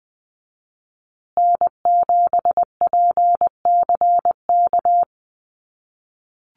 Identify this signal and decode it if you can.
Morse code, heard as a keyed tone, about 20 wpm, 700 Hz. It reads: N7PCK
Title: USA Call Signs Repeated 2x 20wpm